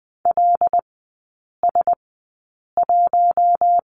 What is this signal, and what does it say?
Morse code, heard as a keyed tone, about 20 wpm, spaced as for 8 wpm, 700 Hz. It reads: LS1
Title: 087 Lesson 15e - Number 5 - Three Characters 20wpm